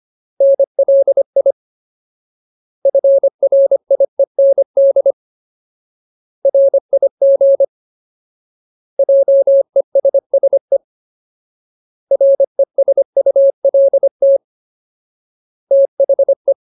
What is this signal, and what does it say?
Morse code, heard as a keyed tone, about 25 wpm, 550 Hz. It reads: NLI FRIEND RIG JESSE RESULT THE
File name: Ultimate Word List / 25WPM Ultimate